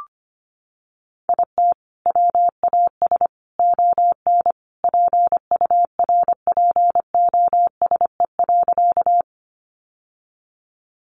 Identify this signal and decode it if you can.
Morse code, heard as a keyed tone, about 25 wpm, 700 Hz. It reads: IT WAS ON PURPOSE.
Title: Sentences from Top 800 Words 25wpm